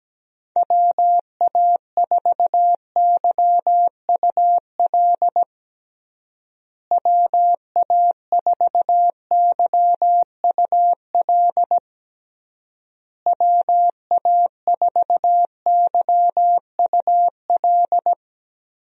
Morse code, heard as a keyed tone, about 17 wpm, 700 Hz. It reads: WA4YUL WA4YUL WA4YUL
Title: USA Call Signs - 6 Characters - Repeated 3x - 17wpm